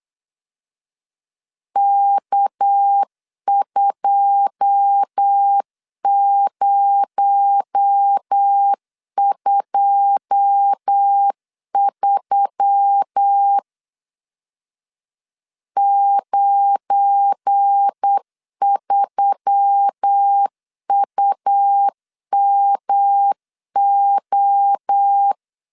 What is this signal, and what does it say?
Morse code, heard as a keyed tone, about 8 wpm, 800 Hz. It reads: K2023 93UMO
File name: LEZIONI / 9 2 8wpm 3437